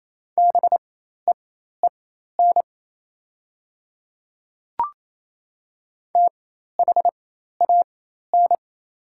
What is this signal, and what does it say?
Morse code, heard as a keyed tone, about 27 wpm, spaced as for 13 wpm, 700 Hz. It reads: BEEN THAN
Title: 4x Character Spacing - Top 300 Words - 28wpm